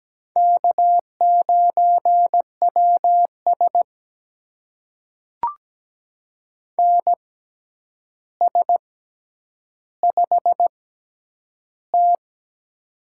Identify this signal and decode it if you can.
Morse code, heard as a keyed tone, about 17 wpm, spaced as for 9 wpm, 700 Hz. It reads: K9WS NS5T
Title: USA Call Signs - 4 Characters - 6x Character Spacing - 17wpm